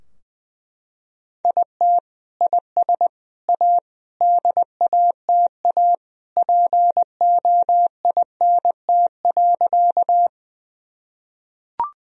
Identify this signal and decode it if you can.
Morse code, heard as a keyed tone, about 20 wpm, 700 Hz. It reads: IT IS A DATA POINT.
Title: Sentences from Top 1000 Words - Review Part 3 - 20wpm